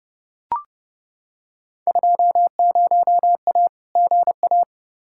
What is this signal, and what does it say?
Morse code, heard as a keyed tone, about 32 wpm, 700 Hz. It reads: 20A GA
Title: Field Day Exchanges Encore - 30wpm